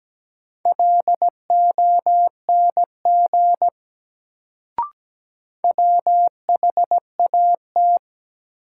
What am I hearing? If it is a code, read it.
Morse code, heard as a keyed tone, about 17 wpm, 700 Hz. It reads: LONG WHAT
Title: Sentences from Top 100 Words - New Word Drill - 17wpm